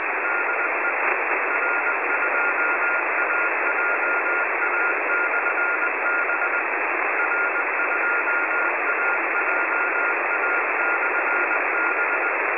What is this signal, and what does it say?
Morse code, heard as a keyed tone, about 20 wpm, 1410 Hz, with radio noise. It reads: DE R2AJA/B R2AJA/B
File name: attachments / websdr recording start 14063.0kHz-x